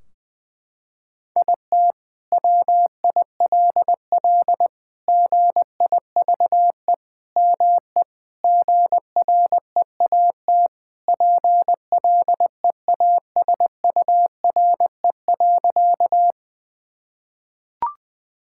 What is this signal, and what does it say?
Morse code, heard as a keyed tone, about 21 wpm, 700 Hz. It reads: IT WILL GIVE ME GREAT PLEASURE.
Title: Sentences from Top 1600 Words 20wpm